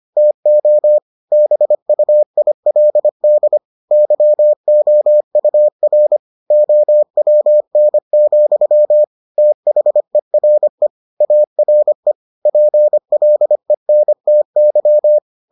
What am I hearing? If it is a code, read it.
Morse code, heard as a keyed tone, about 25 wpm, 600 Hz. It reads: TO BUILD YOUR OWN, THERE ARE PLENTY